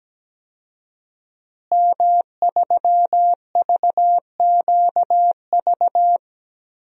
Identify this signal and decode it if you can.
Morse code, heard as a keyed tone, about 17 wpm, 700 Hz. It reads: M3VQV
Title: DXCC-100 Call Signs - 5 Characters - Repeated 3x - 17wpm